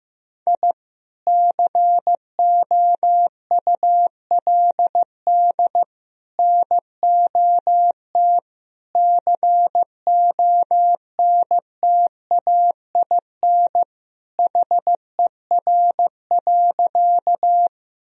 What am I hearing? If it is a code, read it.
Morse code, heard as a keyed tone, about 15 wpm, 700 Hz. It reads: I COULD NOT CONTAIN HER.
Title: Sentences from Top 1000 Words - Review Part 2 - 15wpm